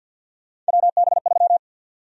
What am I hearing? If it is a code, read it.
Morse code, heard as a keyed tone, about 52 wpm, 700 Hz. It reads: WB3
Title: USA Call Sign Prefixes - 50wpm